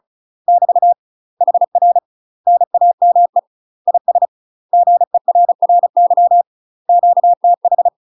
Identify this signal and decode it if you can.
Morse code, heard as a keyed tone, about 34 wpm, 700 Hz. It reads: <BT> HR NAME IS GERRY QTH
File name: UK MASTER QSOs TRAINING/35WPM